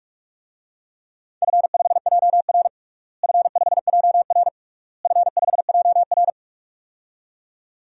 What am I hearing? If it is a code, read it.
Morse code, heard as a keyed tone, about 46 wpm, 700 Hz. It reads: F5JR F5JR F5JR
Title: DXCC-100 Call Signs - 4 Characters - Repeated 3x - 45wpm